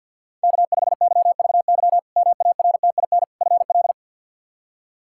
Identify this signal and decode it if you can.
Morse code, heard as a keyed tone, about 50 wpm, 700 Hz. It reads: K5YVY MARTIN FL
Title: K1USN SST QSO Practice - 50wpm